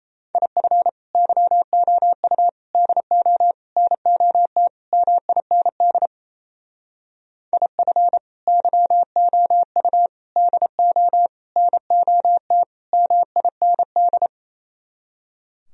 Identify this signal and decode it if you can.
Morse code, heard as a keyed tone, about 35 wpm, 700 Hz. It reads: IF YOU DO NOT MIND IF YOU DO NOT MIND
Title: Speed-Racing - Sets of 5 Words 28wpm